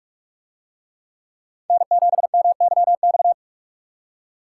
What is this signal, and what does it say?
Morse code, heard as a keyed tone, about 42 wpm, 700 Hz. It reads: N7MYX